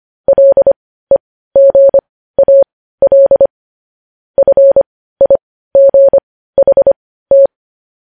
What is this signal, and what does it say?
Morse code, heard as a keyed tone, about 25 wpm, spaced as for 15 wpm, 550 Hz. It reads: LEGAL FIGHT
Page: news headlines in morse code at 15 wpm